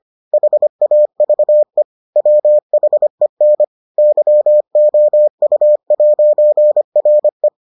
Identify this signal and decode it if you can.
Morse code, heard as a keyed tone, about 25 wpm, 600 Hz. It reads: HAVE WHEN YOU'RE